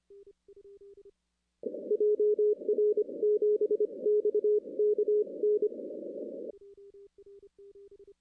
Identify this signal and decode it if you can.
Morse code, heard as a keyed tone, about 25 wpm, 410 Hz. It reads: JR7XKN
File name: 2016 arrl dx cw / 2016-ARRL-DX-CW-K5TR- - 0-26